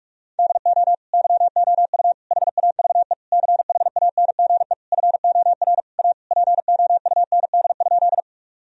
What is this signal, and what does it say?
Morse code, heard as a keyed tone, about 42 wpm, 700 Hz. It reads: DO YOU HAVE CHANGE FOR A POUND?
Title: Sentences from Top 1000 Words - Review Part 2 - 45wpm